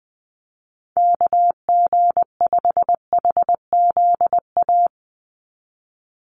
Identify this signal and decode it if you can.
Morse code, heard as a keyed tone, about 20 wpm, 700 Hz. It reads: KG5HZA
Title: USA Call Signs 20wpm